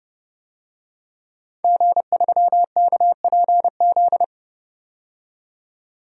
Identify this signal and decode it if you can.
Morse code, heard as a keyed tone, about 32 wpm, 700 Hz. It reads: G3KPZ
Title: DXCC-100 Call Signs - 5 Characters - 30wpm